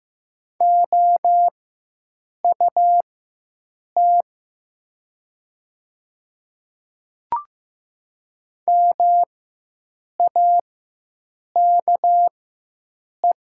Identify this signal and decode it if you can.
Morse code, heard as a keyed tone, about 15 wpm, spaced as for 7 wpm, 700 Hz. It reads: OUT MAKE
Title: 4x Character Spacing - Top 100 Words - 15wpm